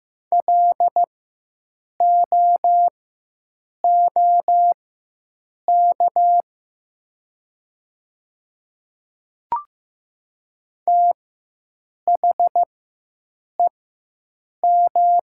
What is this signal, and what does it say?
Morse code, heard as a keyed tone, about 15 wpm, spaced as for 7 wpm, 700 Hz. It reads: LOOK THEM
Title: 4x Character Spacing - Top 200 Words - 15wpm